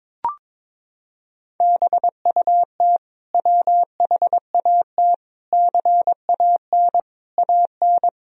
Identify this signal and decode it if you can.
Morse code, heard as a keyed tone, about 22 wpm, 700 Hz. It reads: BUT WHAT CAN AN
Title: Sentences from Top 1000 Words - Review Part 4 - 22wpm